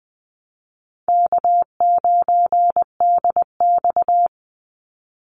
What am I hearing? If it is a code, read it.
Morse code, heard as a keyed tone, about 20 wpm, 700 Hz. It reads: K9DX
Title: Call Signs Repeated 3x - 20wpm